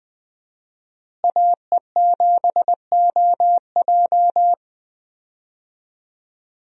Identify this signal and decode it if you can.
Morse code, heard as a keyed tone, about 20 wpm, 700 Hz. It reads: AE7OJ
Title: USA Call Signs - 5 Characters - Repeated 3x - 20wpm